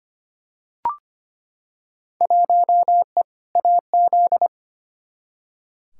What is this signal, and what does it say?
Morse code, heard as a keyed tone, about 25 wpm, 700 Hz. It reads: 1E AZ
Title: Field Day Exchanges - 25wpm